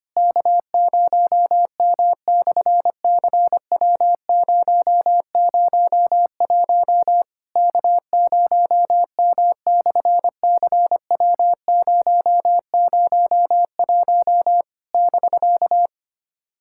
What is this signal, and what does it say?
Morse code, heard as a keyed tone, about 25 wpm, 700 Hz. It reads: K0M/CW001 K0M/CW001 <BK>
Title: SOTA QSO Practice 25wpm